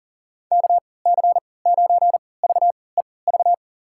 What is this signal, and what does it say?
Morse code, heard as a keyed tone, about 39 wpm, spaced as for 22 wpm, 700 Hz. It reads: KC9VEV
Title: USA Call Signs - 6 Characters - 3x Character Spacing - 40wpm